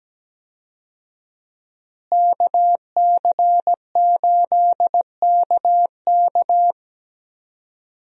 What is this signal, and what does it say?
Morse code, heard as a keyed tone, about 17 wpm, 700 Hz. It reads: KC8KK
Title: USA Call Signs - 5 Characters - Repeated 2x - 17wpm